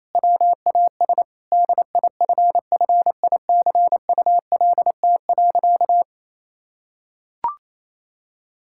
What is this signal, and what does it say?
Morse code, heard as a keyed tone, about 28 wpm, 700 Hz. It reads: WAS DIFFICULT.
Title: Sentences from Top 1000 Words - Review Part 5 - 28wpm